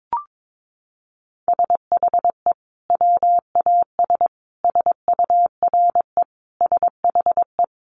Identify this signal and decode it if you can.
Morse code, heard as a keyed tone, about 22 wpm, 700 Hz. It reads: SHE WAS SURE SHE
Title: Sentences from Top 1500 Words 22wpm